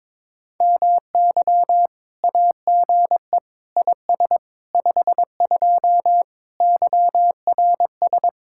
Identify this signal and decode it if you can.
Morse code, heard as a keyed tone, about 23 wpm, 700 Hz. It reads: MY AGE IS 52 YRS
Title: Speed-Racing Common QSO Phrases - 22wpm